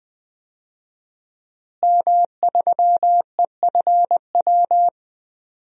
Morse code, heard as a keyed tone, about 20 wpm, 700 Hz. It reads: M3EFW
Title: DXCC-100 Call Signs - 5 Characters - Repeated 2x - 20wpm